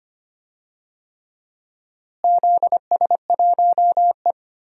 Morse code, heard as a keyed tone, about 25 wpm, 700 Hz. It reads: ZS1E
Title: DXCC-100 Call Signs - 4 Characters - Repeated 3x - 25wpm